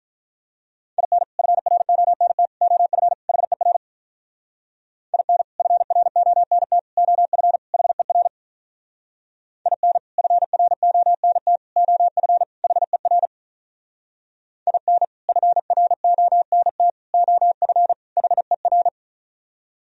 Morse code, heard as a keyed tone, about 45 wpm, 700 Hz. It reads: IN FRONT OF HER IN FRONT OF HER IN FRONT OF HER IN FRONT OF HER